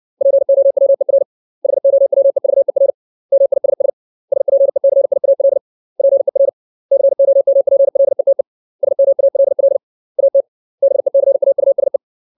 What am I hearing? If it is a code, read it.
Morse code, heard as a keyed tone, about 60 wpm, 550 Hz. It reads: POWER HOWEVER MISS SPECIAL PER COMPLETE SMALL AT BECAUSE
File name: 632 words / 632 Words 60WPM